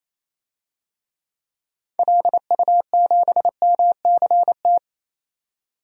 Morse code, heard as a keyed tone, about 28 wpm, 700 Hz. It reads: LU7MCT
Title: DXCC-100 Call Signs - 6 Characters - 28wpm